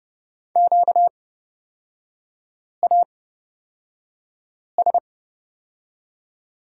Morse code, heard as a keyed tone, about 31 wpm, spaced as for 5 wpm, 690 Hz. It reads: QAS
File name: Lesson 38e - Letter q - Three Characters 30wpm